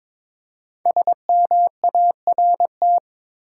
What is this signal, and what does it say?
Morse code, heard as a keyed tone, about 23 wpm, 700 Hz. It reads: SMART
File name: 5 Letter Words - First 3 Letters Given - 22wpm